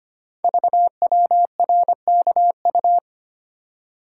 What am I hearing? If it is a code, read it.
Morse code, heard as a keyed tone, about 25 wpm, 700 Hz. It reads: VWRKU
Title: Code Groups - 5 Letter:Numbers - 25wpm